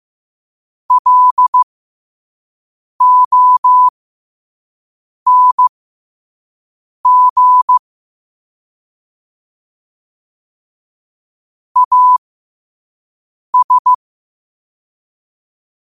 Morse code, heard as a keyed tone, about 15 wpm, spaced as for 5 wpm, 1000 Hz. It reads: LONG AS
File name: qotdfiles / 5wpm